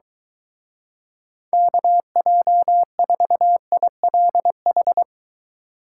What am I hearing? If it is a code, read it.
Morse code, heard as a keyed tone, about 23 wpm, 700 Hz. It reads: KJ4ILH